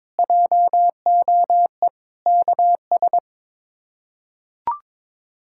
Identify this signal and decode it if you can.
Morse code, heard as a keyed tone, about 22 wpm, 700 Hz. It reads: JOE KS
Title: K1USN SST QSO Practice - 22wpm